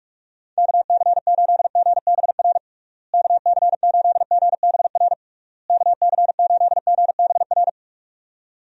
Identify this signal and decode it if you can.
Morse code, heard as a keyed tone, about 46 wpm, 700 Hz. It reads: KC8GBR KC8GBR KC8GBR
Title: USA Call Signs - 6 Characters - Repeated 3x - 45wpm